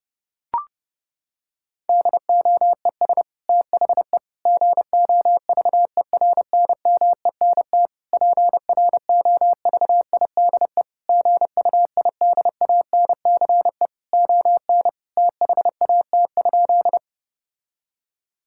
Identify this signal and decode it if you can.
Morse code, heard as a keyed tone, about 32 wpm, 700 Hz. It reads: DOES THE GOVERNMENT PROVIDE GUIDANCE ON THAT?
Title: Sentences from Top 1600 Words 30wpm